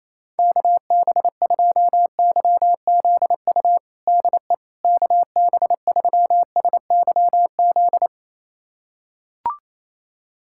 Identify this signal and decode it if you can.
Morse code, heard as a keyed tone, about 28 wpm, 700 Hz. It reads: KB2YZU DE KB3SYZ
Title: USA Callsign DE Callsign 28wpm